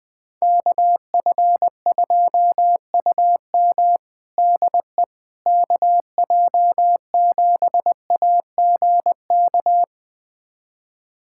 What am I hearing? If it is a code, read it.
Morse code, heard as a keyed tone, about 21 wpm, 700 Hz. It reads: KF2UM DE KJ7AGK
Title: USA Callsign DE Callsign 20wpm